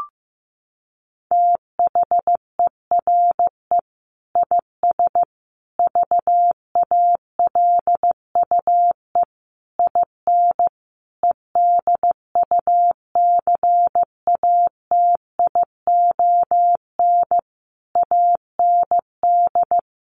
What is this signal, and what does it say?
Morse code, heard as a keyed tone, about 15 wpm, 700 Hz. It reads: THERE IS VALUE IN EDUCATION AND
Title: Sentences from Top 1100 Words 15wpm